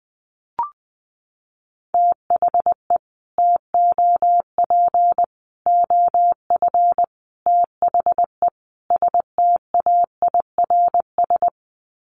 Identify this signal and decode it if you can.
Morse code, heard as a keyed tone, about 20 wpm, 700 Hz. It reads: THE TOP OF THE STAIRS
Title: Sets of 5 Words Encore 20wpm